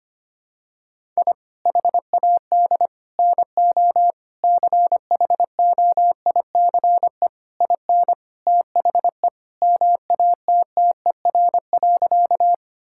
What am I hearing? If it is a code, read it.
Morse code, heard as a keyed tone, about 25 wpm, 700 Hz. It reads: I HAD NO CHOICE IN THE MATTER.
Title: Sentences from Top 1000 Words - Review Part 6 - 25wpm